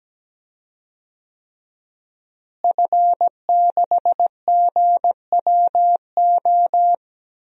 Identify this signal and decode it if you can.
Morse code, heard as a keyed tone, about 17 wpm, 700 Hz. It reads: F6GWO
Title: DXCC-100 Call Signs - 5 Characters - Repeated 3x - 17wpm